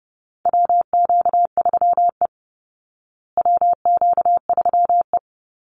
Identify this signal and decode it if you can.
Morse code, heard as a keyed tone, about 30 wpm, 700 Hz. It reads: WQ3E WQ3E
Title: Call Signs Repeated 2x - 30wpm